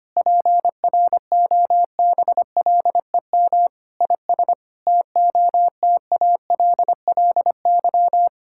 Morse code, heard as a keyed tone, about 25 wpm, 700 Hz. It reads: PROBLEM IS TOTALLY
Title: Sentences from Top 1600 Words 25wpm